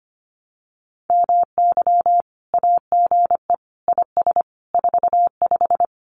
Morse code, heard as a keyed tone, about 25 wpm, 700 Hz. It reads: MY AGE IS 45
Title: Common QSO Phrases Encore 25wpm